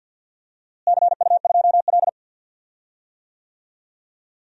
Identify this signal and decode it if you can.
Morse code, heard as a keyed tone, about 47 wpm, 700 Hz. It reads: CU2L
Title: DXCC-100 Call Signs - 4 Characters - Repeated 3x - 50wpm